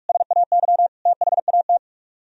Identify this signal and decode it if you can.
Morse code, heard as a keyed tone, about 41 wpm, 700 Hz. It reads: SAY THAT